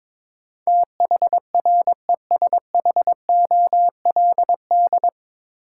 Morse code, heard as a keyed tone, about 22 wpm, 700 Hz. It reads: THRESHOLD
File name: 9 Letter Words - First 3 Letters Given - 22wpm